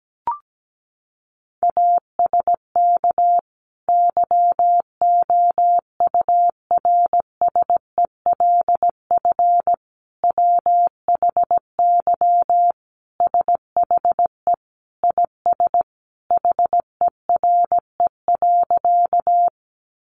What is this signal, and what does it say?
Morse code, heard as a keyed tone, about 17 wpm, 700 Hz. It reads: ASK YOURSELF WHY SHE IS HERE.
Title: Sentences from Top 600 Words 17wpm